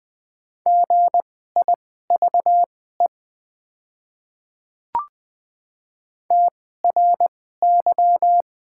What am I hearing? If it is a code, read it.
Morse code, heard as a keyed tone, about 20 wpm, spaced as for 14 wpm, 700 Hz. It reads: GIVE TRY
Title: 2x Character Spacing - Top 300 Words - 20wpm